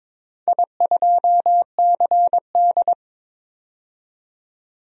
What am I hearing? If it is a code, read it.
Morse code, heard as a keyed tone, about 22 wpm, 700 Hz. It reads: I2CD
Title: DXCC-100 Call Signs - 4 Characters - 22wpm